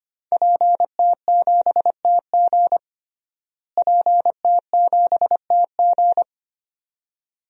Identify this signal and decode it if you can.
Morse code, heard as a keyed tone, about 25 wpm, 700 Hz. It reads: PT7TG PT7TG
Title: DXCC-100 Call Signs - 5 Characters - Repeated 2x - 25wpm